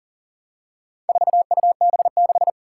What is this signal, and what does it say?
Morse code, heard as a keyed tone, about 40 wpm, 700 Hz. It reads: 4UB6
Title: Four Challenging Letter-Number Combinations 40wpm